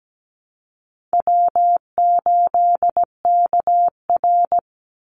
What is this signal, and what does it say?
Morse code, heard as a keyed tone, about 17 wpm, 700 Hz. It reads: W8KR
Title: Easy Call Signs - 17wpm